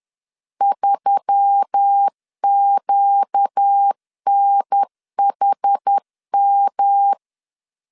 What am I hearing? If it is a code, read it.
Morse code, heard as a keyed tone, about 10 wpm, 800 Hz. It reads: 3QNHM
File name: LEZIONI / 6 PKX0 / 6 PKX 10wpm 2426